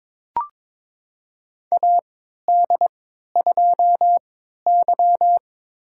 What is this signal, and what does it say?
Morse code, heard as a keyed tone, about 21 wpm, spaced as for 12 wpm, 700 Hz. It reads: AD2Y
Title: USA Call Signs - 4 Characters - 3x Character Spacing - 22wpm